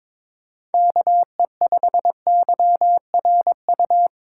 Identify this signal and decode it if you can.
Morse code, heard as a keyed tone, about 21 wpm, 700 Hz. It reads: KE5YRU
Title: USA Call Signs - 6 Characters - 3x Character Spacing - 22wpm